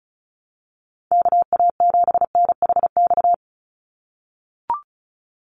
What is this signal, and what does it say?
Morse code, heard as a keyed tone, about 36 wpm, 700 Hz. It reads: KA7NHX
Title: USA Call Signs 35wpm